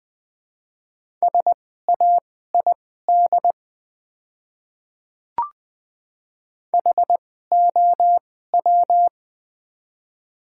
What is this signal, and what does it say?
Morse code, heard as a keyed tone, about 21 wpm, spaced as for 15 wpm, 700 Hz. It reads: SAID HOW
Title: 2x Character Spacing - Top 100 Words - 20wpm